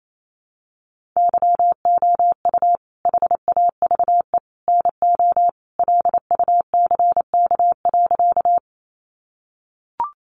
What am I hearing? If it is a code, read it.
Morse code, heard as a keyed tone, about 28 wpm, 700 Hz. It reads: YOU HAVE NO LUCK.
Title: Sentences from Top 1500 Words 28wpm